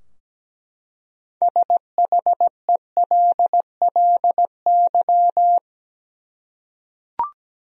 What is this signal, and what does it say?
Morse code, heard as a keyed tone, about 17 wpm, 700 Hz. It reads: SHELLY
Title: USA Common Names - 17wpm